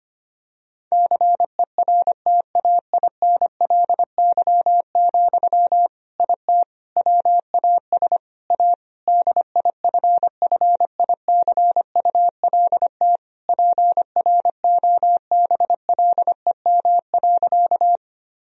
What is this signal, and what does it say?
Morse code, heard as a keyed tone, about 25 wpm, 700 Hz. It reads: CERTAINLY, IT WAS A DIFFICULT PROBLEM.
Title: Sentences from Top 1000 Words - Review Part 2 - 25wpm